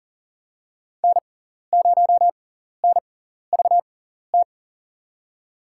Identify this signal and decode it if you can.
Morse code, heard as a keyed tone, about 42 wpm, spaced as for 14 wpm, 700 Hz. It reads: N0NVT
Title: USA Call Signs - 5 Characters - 6x Character Spacing - 40wpm